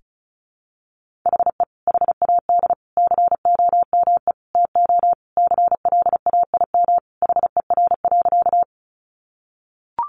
Morse code, heard as a keyed tone, about 34 wpm, 700 Hz. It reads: HE HAD COME TO CLAIM HER.
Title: Sentences from Top 1500 Words 35wpm